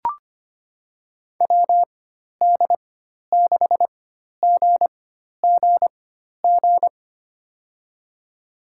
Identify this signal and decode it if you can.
Morse code, heard as a keyed tone, about 25 wpm, spaced as for 12 wpm, 700 Hz. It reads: WD6GGG
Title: USA Call Signs - 6 Characters - 4x Character Spacing - 25wpm